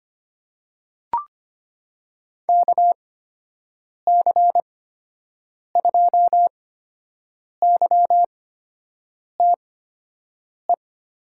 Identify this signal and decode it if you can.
Morse code, heard as a keyed tone, about 25 wpm, spaced as for 7 wpm, 700 Hz. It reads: KC2YTE
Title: USA Call Signs - 6 Characters - 8x Character Spacing - 25wpm